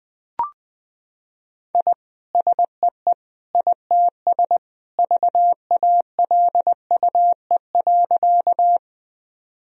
Morse code, heard as a keyed tone, about 20 wpm, 700 Hz. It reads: I SEE ITS VALUE.
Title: Sentences from Top 1000 Words - Review Part 3 - 20wpm